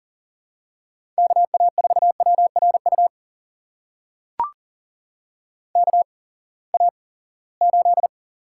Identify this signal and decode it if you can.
Morse code, heard as a keyed tone, about 40 wpm, spaced as for 23 wpm, 700 Hz. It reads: KA4WRU KA8
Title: USA Call Signs - 6 Characters - 8x Character Spacing - 40wpm